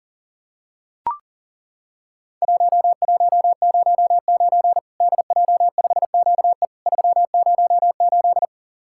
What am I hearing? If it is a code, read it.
Morse code, heard as a keyed tone, about 40 wpm, 700 Hz. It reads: 1109 DJ5QE 308
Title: Worked All Europe DX Contest - QTC Exchange - 40wpm